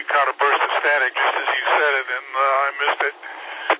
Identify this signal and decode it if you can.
Emergency Alert System (SAME) tones, heard as harsh buzzing bursts, with speech. It is heard from 0.8 to 3.4 s.